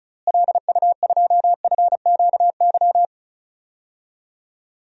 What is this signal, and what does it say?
Morse code, heard as a keyed tone, about 34 wpm, 700 Hz. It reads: LU2FQY
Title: DXCC-100 Call Signs - 6 Characters - Repeated 2x - 35wpm